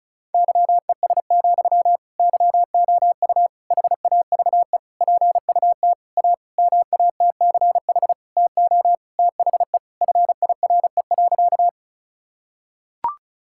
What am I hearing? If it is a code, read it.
Morse code, heard as a keyed tone, about 34 wpm, 700 Hz. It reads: YES, YOU HAVE PUT A MATCH TO THE FIRE.
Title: Sentences from Top 1600 Words 35wpm